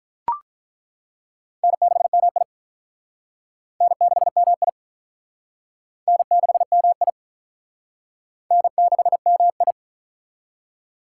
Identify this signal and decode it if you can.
Morse code, heard as a keyed tone, about 41 wpm, 700 Hz. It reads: N6MI N6MI N6MI N6MI
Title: Speed Racing - USA 1x2 Call Signs 35wpm